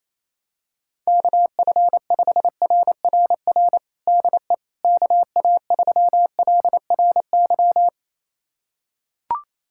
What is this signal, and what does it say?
Morse code, heard as a keyed tone, about 28 wpm, 700 Hz. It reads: KF5RRR DE KA3LRY
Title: USA Callsign DE Callsign - 6 Characters - 28wpm